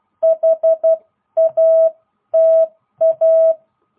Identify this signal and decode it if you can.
Morse code, heard as a keyed tone, about 12 wpm, spaced as for 10 wpm, 660 Hz. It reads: HATA